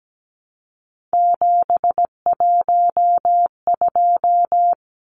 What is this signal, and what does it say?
Morse code, heard as a keyed tone, about 17 wpm, 700 Hz. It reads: 712
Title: Numbers - 3-Digit 17wpm